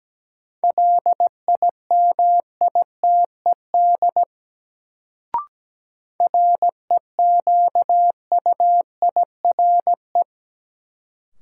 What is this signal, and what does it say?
Morse code, heard as a keyed tone, about 17 wpm, 700 Hz. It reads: LIMITED REQUIRE
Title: Sentences from Top 900 Words - New Word Drill - 17wpm